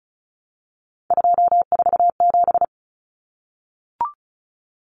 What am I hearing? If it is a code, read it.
Morse code, heard as a keyed tone, about 34 wpm, 700 Hz. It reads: 247
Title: Numbers - 3-Digit 35wpm